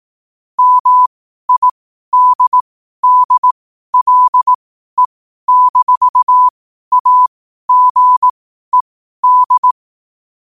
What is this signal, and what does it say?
Morse code, heard as a keyed tone, about 18 wpm, spaced as for 12 wpm, 1000 Hz. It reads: MIDDLE-AGED
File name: qotdfiles / 12wpm